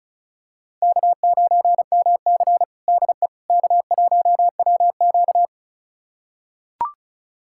Morse code, heard as a keyed tone, about 36 wpm, 700 Hz. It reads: K9MC DE K1WQ
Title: USA Callsign DE Callsign - 4 Characters - 35wpm